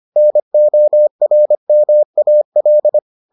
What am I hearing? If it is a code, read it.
Morse code, heard as a keyed tone, about 25 wpm, 600 Hz. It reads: NORMAL